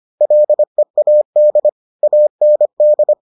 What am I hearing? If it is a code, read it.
Morse code, heard as a keyed tone, about 25 wpm, 600 Hz. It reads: LEAD AND